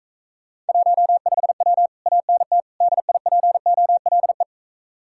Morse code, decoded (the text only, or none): 15W ANT DIPOLE